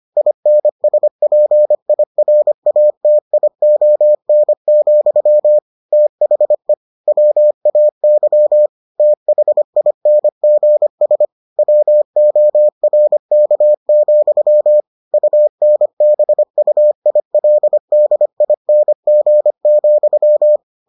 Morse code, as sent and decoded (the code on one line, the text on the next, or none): .. -. ... .--. .. .-. .- - .. --- -. --..-- / - .... / .-- .- -.-- / - .... .. -. --. ... / .-- --- .-. -.- --..-- / ..- -. -... ..- .. .-.. -.. .. -. --. --..--
INSPIRATION, THE WAY THINGS WORK, UNBUILDING,